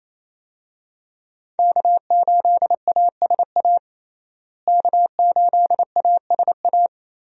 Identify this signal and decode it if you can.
Morse code, heard as a keyed tone, about 28 wpm, 700 Hz. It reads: K8ASA K8ASA
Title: USA Call Signs - 5 Characters - Repeated 3x - 28wpm